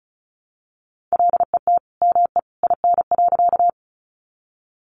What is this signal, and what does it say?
Morse code, heard as a keyed tone, about 34 wpm, 700 Hz. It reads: LET ME IN.
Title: Sentences from Top 300 Words 35wpm